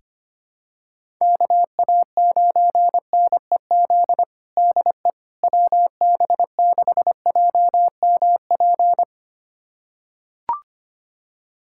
Morse code, heard as a keyed tone, about 25 wpm, 700 Hz. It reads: KA9NEZ DE WB6JMP
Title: USA Callsign DE Callsign 25wpm